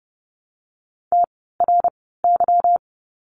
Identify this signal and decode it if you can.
Morse code, heard as a keyed tone, about 30 wpm, spaced as for 17 wpm, 700 Hz. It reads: TRY
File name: course / 30:10wpm / 237 Lesson 38f - Letter q - Words 30wpm